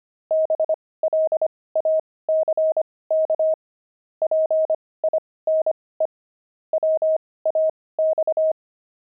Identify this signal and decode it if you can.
Morse code, heard as a keyed tone, about 25 wpm, spaced as for 18 wpm, 630 Hz. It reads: BLACK PINE WAX
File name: pangram-fs-18-cs-25-fragment